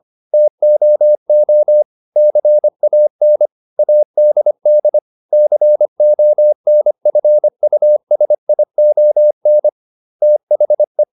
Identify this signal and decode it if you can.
Morse code, heard as a keyed tone, about 25 wpm, 600 Hz. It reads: TOO CAN ADD CONFUSION THE